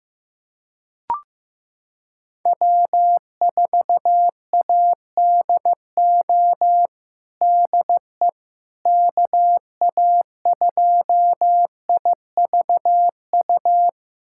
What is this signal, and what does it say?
Morse code, heard as a keyed tone, about 15 wpm, 700 Hz. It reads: W4ADO DE KA2IVU
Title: USA Callsign DE Callsign 15wpm